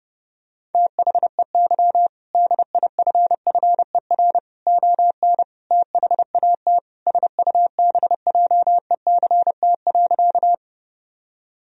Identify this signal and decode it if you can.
Morse code, heard as a keyed tone, about 30 wpm, 700 Hz. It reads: THEY DIFFER ON THAT SUBJECT.
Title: Sentences from Top 1000 Words - Review Part 1 - 30wpm